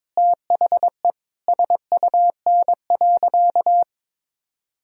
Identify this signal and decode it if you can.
Morse code, heard as a keyed tone, about 22 wpm, 700 Hz. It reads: THE SUN.
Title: Sentences from Top 1000 Words - Review Part 1 - 22wpm